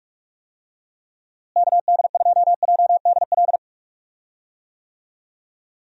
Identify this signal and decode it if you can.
Morse code, heard as a keyed tone, about 46 wpm, 700 Hz. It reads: KD2JDL
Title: USA Call Signs - 6 Characters - Repeated 3x - 45wpm